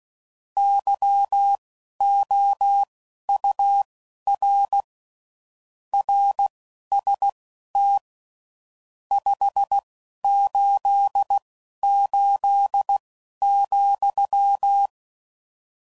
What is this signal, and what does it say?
Morse code, heard as a keyed tone, about 16 wpm, spaced as for 11 wpm, 800 Hz. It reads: YOUR RST 588,
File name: CW Code Groups @10WPM / RSTs / Your RST's